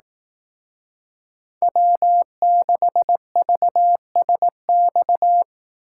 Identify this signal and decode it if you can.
Morse code, heard as a keyed tone, about 19 wpm, 700 Hz. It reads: W6VSX
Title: Speed Racing - USA 1x3 Call Signs 15wpm